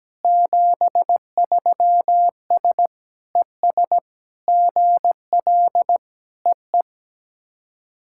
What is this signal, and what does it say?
Morse code, heard as a keyed tone, about 17 wpm, 700 Hz. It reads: 73S ES GL EE
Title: SOTA QSO Practice 17wpm